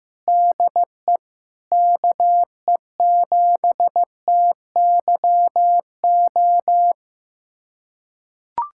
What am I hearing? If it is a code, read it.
Morse code, heard as a keyed tone, about 15 wpm, 700 Hz. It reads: DE KE7TYO
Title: USA Callsign DE Callsign 15wpm